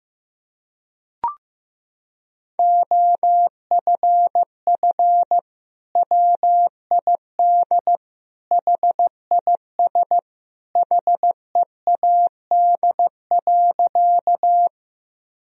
Morse code, heard as a keyed tone, about 15 wpm, 700 Hz. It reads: OFF WID HIS HEAD.
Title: Sentences from Top 300 Words and Abbreviations 15wpm